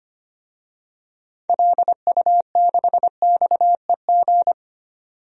Morse code, heard as a keyed tone, about 25 wpm, 700 Hz. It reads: LU6XEG
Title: DXCC-100 Call Signs - 6 Characters - 25wpm